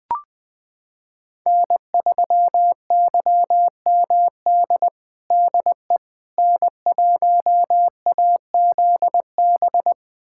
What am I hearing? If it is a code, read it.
Morse code, heard as a keyed tone, about 20 wpm, 700 Hz. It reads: N3YMD DE N1AZB
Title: USA Callsign DE Callsign - 5 Characters - 20wpm